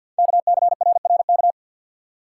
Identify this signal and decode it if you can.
Morse code, heard as a keyed tone, about 48 wpm, 700 Hz. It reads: KCRRK
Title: Code Groups - 5 Letter:Numbers - 50wpm